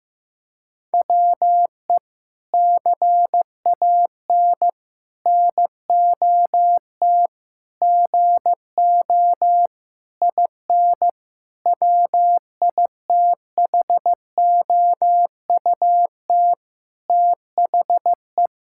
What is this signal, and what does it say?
Morse code, heard as a keyed tone, about 15 wpm, 700 Hz. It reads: WE CAN NOT GO IN WITHOUT THE
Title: Sentences from Top 1000 Words - Review Part 6 - 15wpm